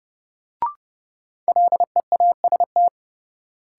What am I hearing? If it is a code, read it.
Morse code, heard as a keyed tone, about 29 wpm, 700 Hz. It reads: LEAST
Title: Sentences from Top 700 Words - New Word Drill - 30wpm